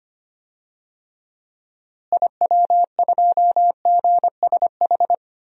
Morse code, heard as a keyed tone, about 25 wpm, 700 Hz. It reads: IW2GSH
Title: DXCC-100 Call Signs - 6 Characters - Repeated 3x - 25wpm